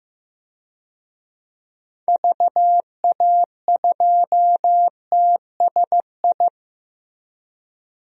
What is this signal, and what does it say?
Morse code, heard as a keyed tone, about 15 wpm, 700 Hz. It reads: VA2TSI
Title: DXCC-100 Call Signs - 6 Characters - Repeated 2x - 15wpm